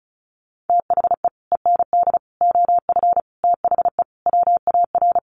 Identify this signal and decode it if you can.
Morse code, heard as a keyed tone, about 34 wpm, 700 Hz. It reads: THE END OF THE WAR
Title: Sets of 5 Words 35wpm